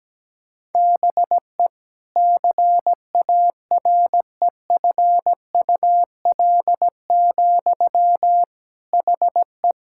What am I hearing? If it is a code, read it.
Morse code, heard as a keyed tone, about 17 wpm, 700 Hz. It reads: BE CAREFUL, HE
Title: Sentences from Top 1600 Words 17wpm